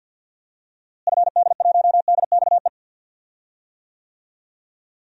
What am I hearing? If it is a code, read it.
Morse code, heard as a keyed tone, about 50 wpm, 700 Hz. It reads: FD1DXE
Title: DXCC-100 Call Signs - 6 Characters - Repeated 3x - 50wpm